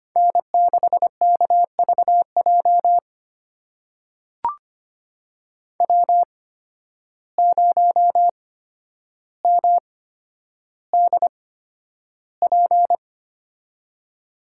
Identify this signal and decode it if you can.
Morse code, heard as a keyed tone, about 25 wpm, spaced as for 11 wpm, 700 Hz. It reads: N6KVJ W0MDP
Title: USA Call Signs - 5 Characters - 8x Character Spacing - 25wpm